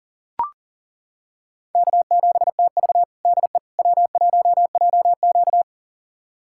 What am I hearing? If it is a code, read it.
Morse code, heard as a keyed tone, about 37 wpm, 700 Hz. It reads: K7TV DE W1JQ
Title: USA Callsign DE Callsign - 4 Characters - 40wpm